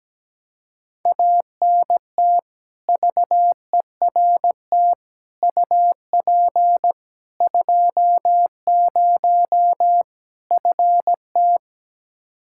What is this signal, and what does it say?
Morse code, heard as a keyed tone, about 17 wpm, 700 Hz. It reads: ANT VERT UP 20 FT
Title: Practice QSO - Exchange 2 - Info - 17wpm